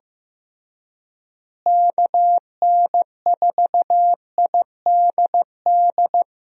Morse code, heard as a keyed tone, about 15 wpm, 700 Hz. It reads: KN4IDD
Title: USA Call Signs - 6 Characters - Repeated 3x - 15wpm